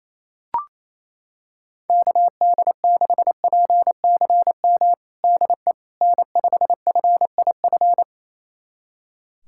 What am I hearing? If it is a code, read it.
Morse code, heard as a keyed tone, about 28 wpm, 700 Hz. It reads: KD6PCM DE N5FIF
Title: USA Callsign DE Callsign 28wpm